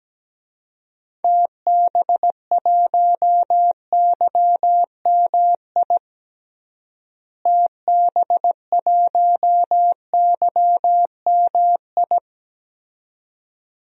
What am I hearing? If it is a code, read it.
Morse code, heard as a keyed tone, about 17 wpm, 700 Hz. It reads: TB1YMI TB1YMI
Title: DXCC-100 Call Signs - 6 Characters - Repeated 2x - 17wpm